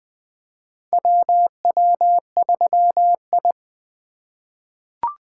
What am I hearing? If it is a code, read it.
Morse code, heard as a keyed tone, about 20 wpm, 700 Hz. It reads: WW3I
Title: USA Call Signs - 4 Characters - 3x Character Spacing - 20wpm